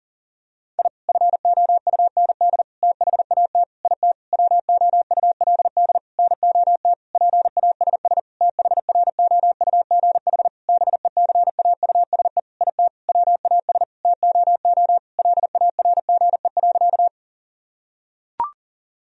Morse code, heard as a keyed tone, about 40 wpm, 700 Hz. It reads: I FOUND THAT IT WOULD NOT PASS THROUGH BECAUSE IT WAS TOO LARGE.
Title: Sentences from Top 1000 Words - Review Part 1 - 40wpm